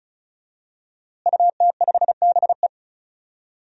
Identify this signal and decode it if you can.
Morse code, heard as a keyed tone, about 36 wpm, 700 Hz. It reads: UT5BE
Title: DXCC-100 Call Signs - 5 Characters - Repeated 2x - 35wpm